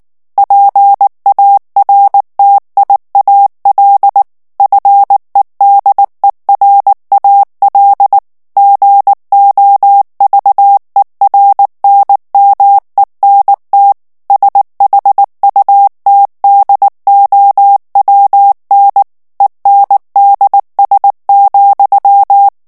Morse code, heard as a keyed tone, about 19 wpm, 800 Hz. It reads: PARTIAL FEDERAL GOVERNMENT SHUTDOWN ENDS,
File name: ANGLAIS texte clair 3 vitesse 18 mots minute